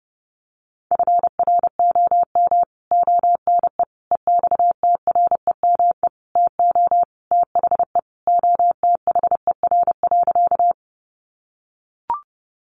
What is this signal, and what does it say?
Morse code, heard as a keyed tone, about 30 wpm, 700 Hz. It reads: FROM ONE EXTREME TO THE OTHER.
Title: Sentences from Top 1300 Words 30wpm